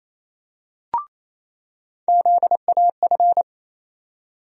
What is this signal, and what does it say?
Morse code, heard as a keyed tone, about 28 wpm, 700 Hz. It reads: ZAF
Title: USA Call Sign Postfixes - 3 Letters 28wpm